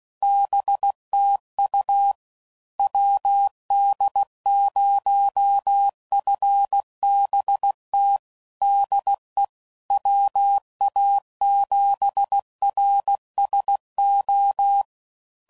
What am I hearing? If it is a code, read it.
Morse code, heard as a keyed tone, about 16 wpm, 800 Hz. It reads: BTU WD0FBT DE WA7RSO
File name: CW Code Groups @15WPM / QSOs / WD0FBT Spaced